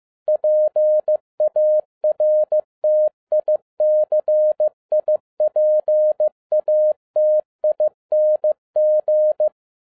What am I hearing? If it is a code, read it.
Morse code, heard as a keyed tone, about 15 wpm, 610 Hz. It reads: PARTICIPATING
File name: 2011-10-mp3 / n12-JL3YMV 15wpm